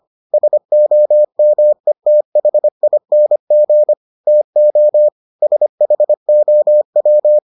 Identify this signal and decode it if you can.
Morse code, heard as a keyed tone, about 25 wpm, 600 Hz. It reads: SOMETHING TO SHOW